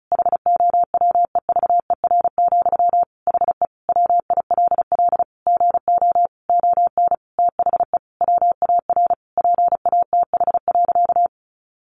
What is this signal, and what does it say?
Morse code, heard as a keyed tone, about 34 wpm, 700 Hz. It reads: HOWEVER, HE WILL GO ON THE WAR PATH.
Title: Sentences from Top 1400 Words 35wpm